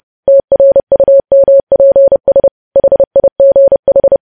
Morse code, heard as a keyed tone, about 30 wpm, 550 Hz. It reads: TRUMPS HIGH
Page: news headlines in morse code at 30 wpm